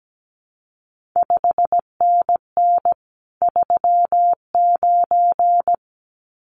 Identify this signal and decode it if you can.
Morse code, heard as a keyed tone, about 17 wpm, 700 Hz. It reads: 5NN 39
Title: CQ WW DX Contest Exchange Encore - 17wpm